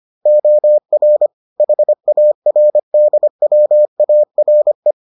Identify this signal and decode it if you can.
Morse code, heard as a keyed tone, about 25 wpm, 600 Hz. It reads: OR HARDWARE